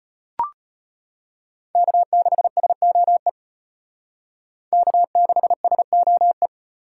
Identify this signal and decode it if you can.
Morse code, heard as a keyed tone, about 33 wpm, 700 Hz. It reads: K6SOE K6SOE